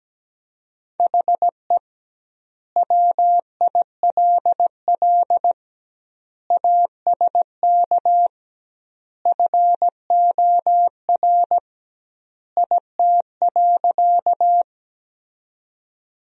Morse code, heard as a keyed tone, about 17 wpm, 700 Hz. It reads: HE WILL ASK FOR IT.
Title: 2x Word-Spacing - Sentences from Top 200 Words - 17wpm